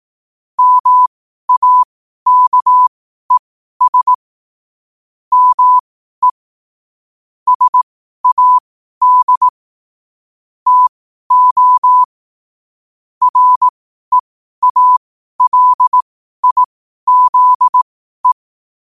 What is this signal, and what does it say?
Morse code, heard as a keyed tone, about 18 wpm, spaced as for 12 wpm, 1000 Hz. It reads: MAKES ME SAD TO REALIZE
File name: qotdfiles / 12wpm